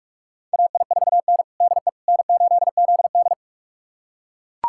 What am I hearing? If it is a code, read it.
Morse code, heard as a keyed tone, about 42 wpm, 700 Hz. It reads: AI4N DE N8ZD
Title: USA Callsign DE Callsign - 4 Characters - 45wpm